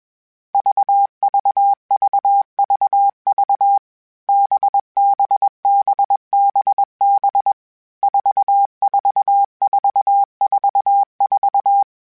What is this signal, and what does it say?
Morse code, heard as a keyed tone, about 21 wpm, 800 Hz. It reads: VVVVV BBBBB 44444
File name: Basic Sets / Basic Letter Groups Collective7 / VB46@20